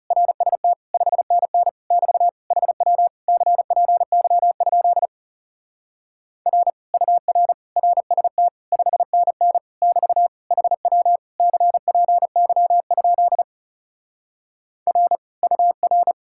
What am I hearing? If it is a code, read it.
Morse code, heard as a keyed tone, about 40 wpm, 700 Hz. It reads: RST 5NN <BT> HW CPY? R UR RST 5NN <BT> HW CPY? R UR